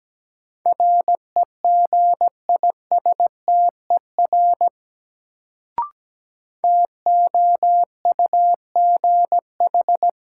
Morse code, heard as a keyed tone, about 17 wpm, 700 Hz. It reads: REGISTER TOUGH
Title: Sentences from Top 1200 Words - New Word Drill - 17wpm